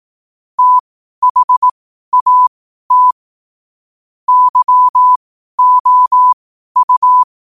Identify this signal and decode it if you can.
Morse code, heard as a keyed tone, about 18 wpm, spaced as for 12 wpm, 1000 Hz. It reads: THAT YOU
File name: qotdfiles / 12wpm